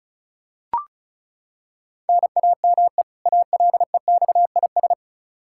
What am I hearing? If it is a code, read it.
Morse code, heard as a keyed tone, about 36 wpm, 700 Hz. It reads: NAME ALEXIS
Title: Practice QSO - Exchange 1 - Beginning - 35wpm